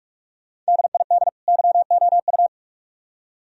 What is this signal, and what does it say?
Morse code, heard as a keyed tone, about 56 wpm, 690 Hz. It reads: DID YOU
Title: Speed-Racing - Sets of 2 Words 45wpm